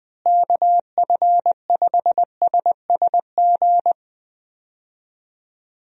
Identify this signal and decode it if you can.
Morse code, heard as a keyed tone, about 20 wpm, 700 Hz. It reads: KF5SSG
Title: USA Call Signs - 6 Characters - Repeated 3x - 20wpm